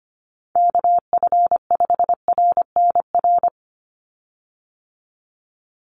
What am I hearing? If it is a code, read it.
Morse code, heard as a keyed tone, about 25 wpm, 700 Hz. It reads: KF5RNR
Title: USA Call Signs 25wpm